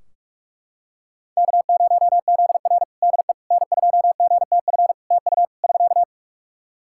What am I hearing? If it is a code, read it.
Morse code, heard as a keyed tone, about 44 wpm, 700 Hz. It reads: K0ZR DE N2GTF TU <SK>
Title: Practice QSO - Exchange 4 - Ending - 45wpm